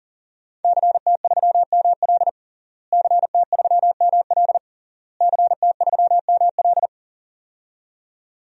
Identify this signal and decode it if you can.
Morse code, heard as a keyed tone, about 40 wpm, 700 Hz. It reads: CT3ML CT3ML CT3ML
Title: DXCC-100 Call Signs - 5 Characters - Repeated 3x - 40wpm